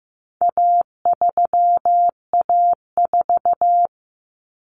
Morse code, heard as a keyed tone, about 15 wpm, 700 Hz. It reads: A3A4
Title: Common Cut Numbers - 4-Digit 15wpm